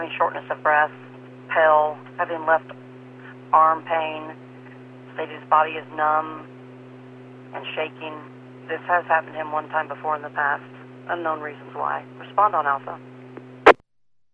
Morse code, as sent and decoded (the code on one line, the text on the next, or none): none